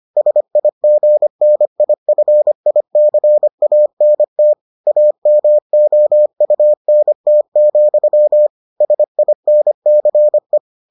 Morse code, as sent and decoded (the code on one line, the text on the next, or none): ... .. --. -. .. ..-. .. -.-. .- -. - / .- -- --- ..- -. - --..-- / ... .. -. -.-. .
SIGNIFICANT AMOUNT, SINCE